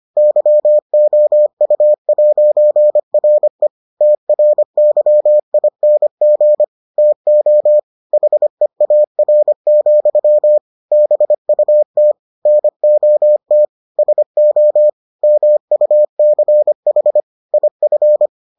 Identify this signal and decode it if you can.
Morse code, heard as a keyed tone, about 25 wpm, 600 Hz. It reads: YOU'RE TRYING TO HEAR, BUT NOT SO MUCH IF